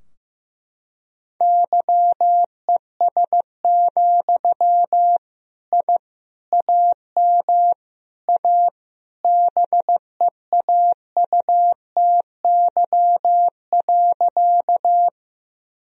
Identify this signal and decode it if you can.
Morse code, heard as a keyed tone, about 15 wpm, 700 Hz. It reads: YES, I AM A BEAUTY.
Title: Sentences from Top 1000 Words - Review Part 2 - 15wpm